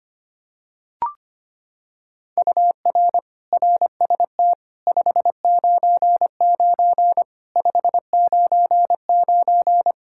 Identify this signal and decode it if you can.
Morse code, heard as a keyed tone, about 25 wpm, 700 Hz. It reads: UR RST 599 599
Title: Practice QSO - Exchange 1 - Beginning - 25wpm